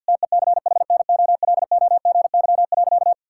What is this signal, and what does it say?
Morse code, heard as a keyed tone, about 50 wpm, 700 Hz. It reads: TECHNOLOGY.
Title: Sentences from Top 1000 Words - Review Part 5 - 50wpm